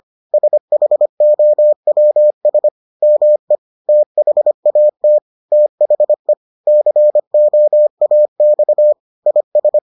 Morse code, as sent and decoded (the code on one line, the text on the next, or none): ... .... --- .-- ... / -- . / - .... .- - / - .... / -.-. --- .- -..- / .. ...
SHOWS ME THAT THE COAX IS